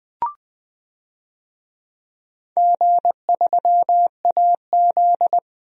Morse code, heard as a keyed tone, about 20 wpm, 700 Hz. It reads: G3AZ
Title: DXCC-100 Call Signs - 4 Characters - 20wpm